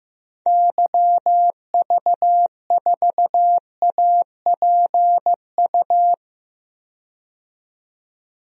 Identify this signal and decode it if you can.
Morse code, heard as a keyed tone, about 15 wpm, 700 Hz. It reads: YV4APU
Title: DXCC-100 Call Signs - 6 Characters - 15wpm